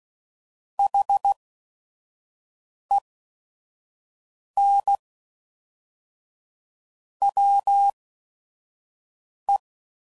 Morse code, heard as a keyed tone, about 16 wpm, spaced as for 5 wpm, 800 Hz. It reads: HEN WE
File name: CW Code Groups @5WPM / QSOs / QSO WD0FBT